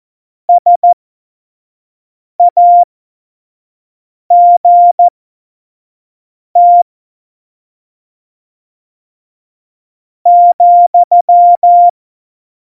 Morse code, heard as a keyed tone, about 14 wpm, spaced as for 4 wpm, 700 Hz. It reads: SAGT,
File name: Morsekurs / Witze / Witz 06 March 2018, 25 F70